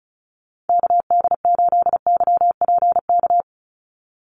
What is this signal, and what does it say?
Morse code, heard as a keyed tone, about 34 wpm, 700 Hz. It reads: KD8YPK
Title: USA Call Signs 35wpm